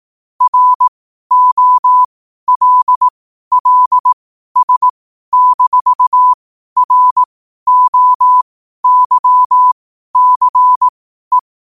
Morse code, heard as a keyed tone, about 18 wpm, spaced as for 12 wpm, 1000 Hz. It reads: ROLLS-ROYCE
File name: qotdfiles / 12wpm